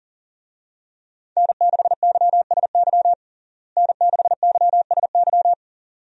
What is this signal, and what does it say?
Morse code, heard as a keyed tone, about 37 wpm, 700 Hz. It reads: N6YSY N6YSY
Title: USA Call Signs - 5 Characters - Repeated 2x - 40wpm